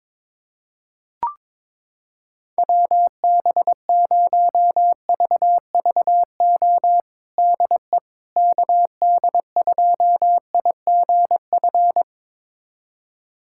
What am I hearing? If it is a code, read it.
Morse code, heard as a keyed tone, about 22 wpm, 700 Hz. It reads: WB0VVO DE KD2IGF
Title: USA Callsign DE Callsign - 6 Characters - 22wpm